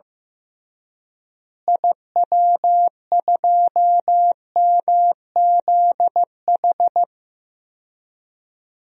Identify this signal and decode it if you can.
Morse code, heard as a keyed tone, about 15 wpm, 700 Hz. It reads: IW2MZH
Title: DXCC-100 Call Signs - 6 Characters - Repeated 3x - 15wpm